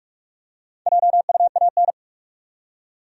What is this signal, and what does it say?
Morse code, heard as a keyed tone, about 46 wpm, 700 Hz. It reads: JUAN
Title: USA Common Names - Max 5 Letters - 45wpm